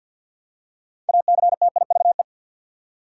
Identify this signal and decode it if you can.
Morse code, heard as a keyed tone, about 52 wpm, 700 Hz. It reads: ACTIVE
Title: Six Letter Words - 50wpm